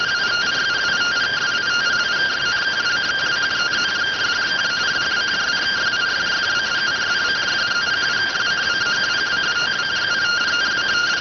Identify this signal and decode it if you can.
Two-tone teleprinter signal, about 170 Hz shift.